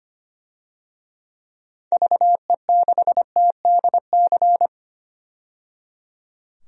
Morse code, heard as a keyed tone, about 25 wpm, 700 Hz. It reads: VE6TDC
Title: DXCC-100 Call Signs - 6 Characters - 25wpm